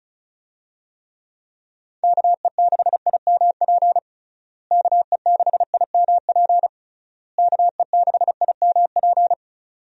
Morse code, heard as a keyed tone, about 34 wpm, 700 Hz. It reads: KE6IMP KE6IMP KE6IMP
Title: USA Call Signs - 6 Characters - Repeated 3x - 35wpm